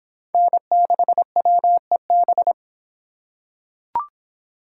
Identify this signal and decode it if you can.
Morse code, heard as a keyed tone, about 24 wpm, 700 Hz. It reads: N6WEB